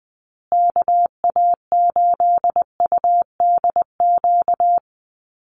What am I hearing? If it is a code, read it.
Morse code, heard as a keyed tone, about 19 wpm, 700 Hz. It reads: KA8UDQ
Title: USA Call Signs Repeated 3x 20wpm